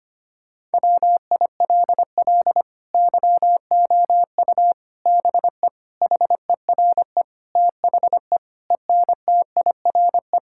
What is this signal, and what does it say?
Morse code, heard as a keyed tone, about 25 wpm, 700 Hz. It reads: WILL YOU BE HERE THE ENTIRE
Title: Sentences from Top 1000 Words - Review Part 5 - 25wpm